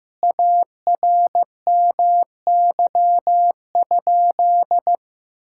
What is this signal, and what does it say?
Morse code, heard as a keyed tone, about 15 wpm, 700 Hz. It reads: ARMY?
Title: Sentences from Top 1000 Words - Review Part 6 - 15wpm